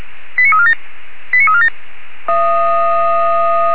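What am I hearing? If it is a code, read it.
Five-tone selective call (ZVEI), heard as quick runs of five tones, tones of about 70 ms. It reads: 89347 89347
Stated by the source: ZVEI five-tone selective call 89155 received on an SDR.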